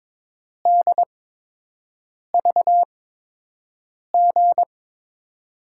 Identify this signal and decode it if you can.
Morse code, heard as a keyed tone, about 22 wpm, spaced as for 6 wpm, 700 Hz. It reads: DVG
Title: USA Call Signs - 6 Characters - 8x Character Spacing - 22wpm